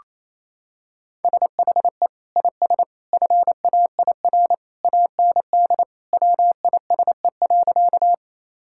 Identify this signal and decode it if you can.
Morse code, heard as a keyed tone, about 28 wpm, 700 Hz. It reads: SHE IS FAIR AND WISE.
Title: Sentences from Top 1600 Words 28wpm